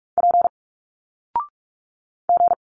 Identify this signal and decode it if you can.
Morse code, heard as a keyed tone, about 46 wpm, 700 Hz. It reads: P G
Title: Single Letters 45wpm